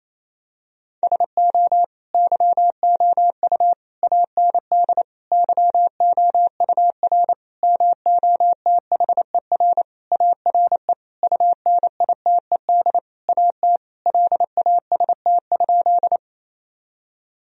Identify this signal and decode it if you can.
Morse code, heard as a keyed tone, about 27 wpm, 700 Hz. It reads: SO YOU AND YOUR MOTHER ARE UNITED AT LAST?
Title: Sentences from Top 1000 Words - Review Part 4 - 28wpm